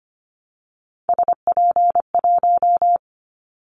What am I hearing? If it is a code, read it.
Morse code, heard as a keyed tone, about 25 wpm, 700 Hz. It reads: SP1
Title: Callsign Prefixes for Top 25 DXCC - Repeated 3x - 25wpm